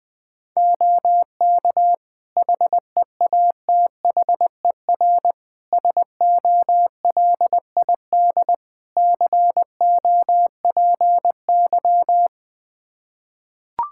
Morse code, heard as a keyed tone, about 21 wpm, 700 Hz. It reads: OK HEATHER SOLID COPY
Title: K1USN SST QSO Practice 20wpm